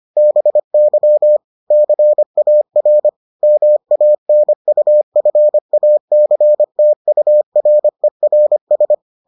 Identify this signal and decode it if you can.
Morse code, heard as a keyed tone, about 25 wpm, 600 Hz. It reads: BY CAR MANUFACTURERS